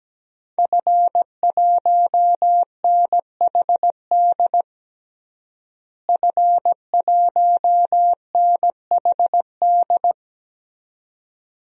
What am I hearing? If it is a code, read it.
Morse code, heard as a keyed tone, about 17 wpm, 700 Hz. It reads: F1NHD F1NHD
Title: DXCC-100 Call Signs - 5 Characters - Repeated 2x - 17wpm